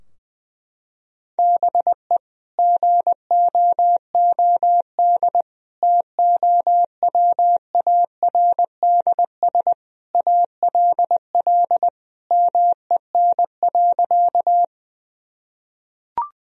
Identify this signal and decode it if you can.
Morse code, heard as a keyed tone, about 19 wpm, 700 Hz. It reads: BE GOOD TOWARDS ALL MEN.
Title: Sentences from Top 1000 Words - Review Part 2 - 20wpm